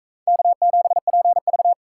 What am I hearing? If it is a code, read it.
Morse code, heard as a keyed tone, about 43 wpm, 700 Hz. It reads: K7PV